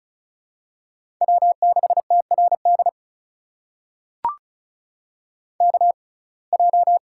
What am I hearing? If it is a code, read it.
Morse code, heard as a keyed tone, about 36 wpm, spaced as for 26 wpm, 700 Hz. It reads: W6TRD KJ
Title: USA Call Signs - 6x Character Spacing - 35wpm